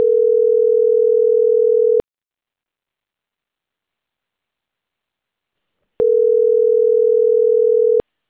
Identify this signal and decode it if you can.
Ringback tone (North America), 440 + 480 Hz.